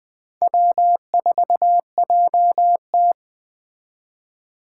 Morse code, heard as a keyed tone, about 18 wpm, 700 Hz. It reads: W4JT
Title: Speed Racing - USA 1x2 Call Signs 15wpm